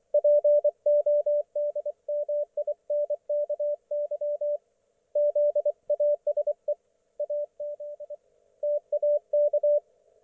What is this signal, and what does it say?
Morse code, heard as a keyed tone, about 24 wpm, 590 Hz, with radio noise. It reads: PODMINKY ZASE AZ TAK
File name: Archiv / CUC212 / ok1fhi cul212